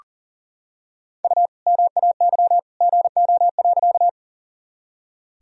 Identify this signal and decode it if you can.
Morse code, heard as a keyed tone, about 40 wpm, 700 Hz. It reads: U MAY GO.
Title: Sentences from Top 100 Words and Abbreviations 40wpm